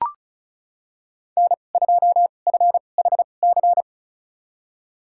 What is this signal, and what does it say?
Morse code, heard as a keyed tone, about 36 wpm, spaced as for 26 wpm, 700 Hz. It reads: N2FHC
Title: USA Call Signs - 5 Characters - 2x Character Spacing - 35wpm